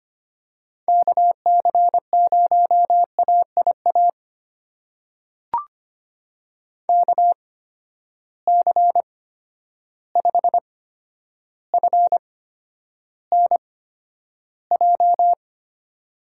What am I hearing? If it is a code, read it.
Morse code, heard as a keyed tone, about 25 wpm, spaced as for 11 wpm, 700 Hz. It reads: KC0AIA KC5FNJ
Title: USA Call Signs - 6 Characters - 8x Character Spacing - 25wpm